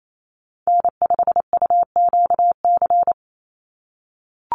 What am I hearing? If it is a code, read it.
Morse code, heard as a keyed tone, about 28 wpm, 700 Hz. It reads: N5UQC
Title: USA Call Signs 28wpm